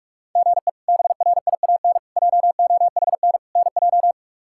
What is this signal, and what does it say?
Morse code, heard as a keyed tone, about 45 wpm, 700 Hz. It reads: GE BRIAN JOHN NJ
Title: K1USN SST QSO Practice - 45wpm